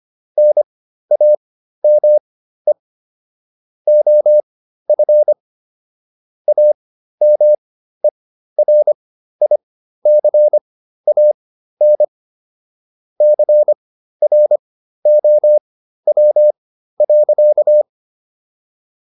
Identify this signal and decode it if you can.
Morse code, heard as a keyed tone, about 25 wpm, spaced as for 13 wpm, 600 Hz. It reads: NAME OF AMERICAN CROW.